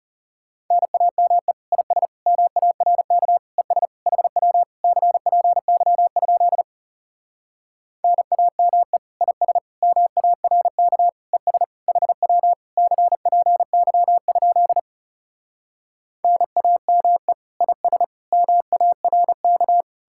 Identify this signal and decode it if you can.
Morse code, heard as a keyed tone, about 40 wpm, 700 Hz. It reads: NAME IS MARK ES HW CPY? NAME IS MARK ES HW CPY? NAME IS MARK